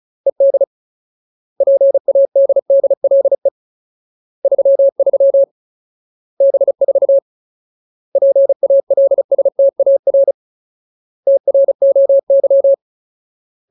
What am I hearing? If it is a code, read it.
Morse code, heard as a keyed tone, about 34 wpm, 550 Hz. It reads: ED PADDLE 33 B4 PALSTAR TROY
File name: Ultimate Word List / 35WPM Ultimate